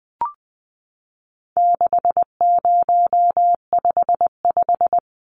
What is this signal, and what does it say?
Morse code, heard as a keyed tone, about 20 wpm, 700 Hz. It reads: 6055
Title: Numbers - 4-Digit 20wpm